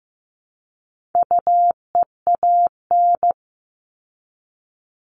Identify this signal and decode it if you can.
Morse code, heard as a keyed tone, about 15 wpm, 700 Hz. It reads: UEAN
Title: All Cut Numbers - 4-Digit 15wpm